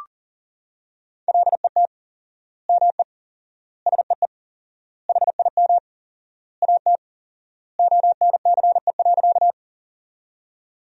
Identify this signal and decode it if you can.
Morse code, heard as a keyed tone, about 40 wpm, 700 Hz. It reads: LET ME SEE HIM AT ONCE.
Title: 4x Word-Spacing - Sentences from Top 300 Words - 40wpm